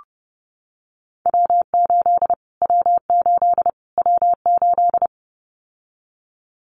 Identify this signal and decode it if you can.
Morse code, heard as a keyed tone, about 30 wpm, 700 Hz. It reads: W8 W8 W8
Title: Callsign Prefixes for Top 25 DXCC - Repeated 3x - 30wpm